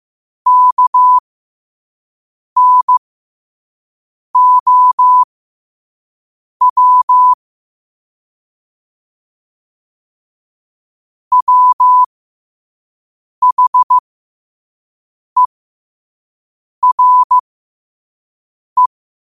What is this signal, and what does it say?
Morse code, heard as a keyed tone, about 15 wpm, spaced as for 5 wpm, 1000 Hz. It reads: KNOW WHERE